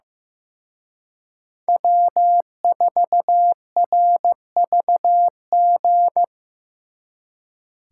Morse code, heard as a keyed tone, about 18 wpm, 700 Hz. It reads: W4RVG
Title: Speed Racing - USA 1x3 Call Signs 15wpm